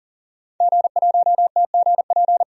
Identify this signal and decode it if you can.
Morse code, heard as a keyed tone, about 40 wpm, 700 Hz. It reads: G1TGP
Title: DXCC-100 Call Signs - 5 Characters - 40wpm